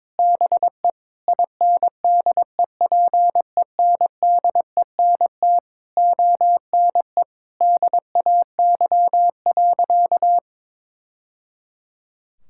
Morse code, heard as a keyed tone, about 22 wpm, 700 Hz. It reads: BE INDEPENDENT ONE DAY.
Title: Sentences from Top 1000 Words - Review Part 6 - 22wpm